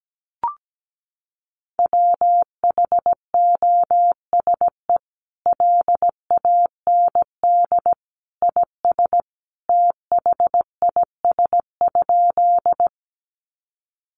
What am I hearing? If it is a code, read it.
Morse code, heard as a keyed tone, about 17 wpm, 700 Hz. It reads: WHOSE LAND IS THIS?
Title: Sentences from Top 1200 Words 17wpm